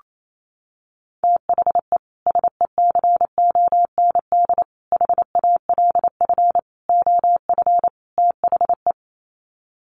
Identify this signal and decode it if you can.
Morse code, heard as a keyed tone, about 28 wpm, 700 Hz. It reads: THE SECOND HALF OF THE
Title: Sets of 5 Words 28wpm